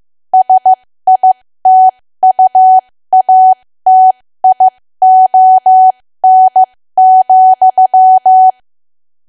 Morse code, heard as a keyed tone, about 15 wpm, spaced as for 13 wpm, 750 Hz. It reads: SITUATION,
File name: cwa / W1AW-Level2 / 13WPM